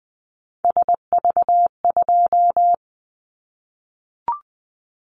Morse code, heard as a keyed tone, about 20 wpm, 700 Hz. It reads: SV2
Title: Callsign Prefixes for Top 50 DXCC - 20wpm